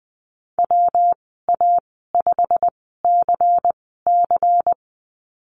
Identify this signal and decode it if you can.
Morse code, heard as a keyed tone, about 20 wpm, spaced as for 15 wpm, 700 Hz. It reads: WA5CC
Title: 161 Lesson 26g - Letter g - Call Signs 20wpm